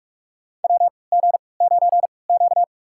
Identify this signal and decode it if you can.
Morse code, heard as a keyed tone, about 41 wpm, spaced as for 24 wpm, 700 Hz. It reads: WG9Q
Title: USA Call Signs - 3x Character Spacing - 45wpm